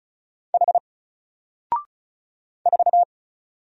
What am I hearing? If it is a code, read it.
Morse code, heard as a keyed tone, about 36 wpm, 700 Hz. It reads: H 4
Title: Single Letter-Number Alt 35wpm